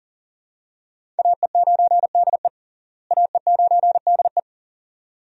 Morse code, heard as a keyed tone, about 40 wpm, 700 Hz. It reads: AE9DE AE9DE
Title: USA Call Signs - 5 Characters - Repeated 2x - 40wpm